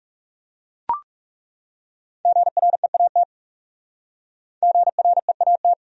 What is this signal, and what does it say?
Morse code, heard as a keyed tone, about 38 wpm, 700 Hz. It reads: GREAT GREAT